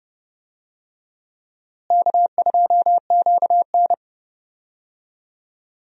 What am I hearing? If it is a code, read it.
Morse code, heard as a keyed tone, about 31 wpm, 700 Hz. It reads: K2QN
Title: USA Call Signs - 4 Characters - Repeated 2x - 30wpm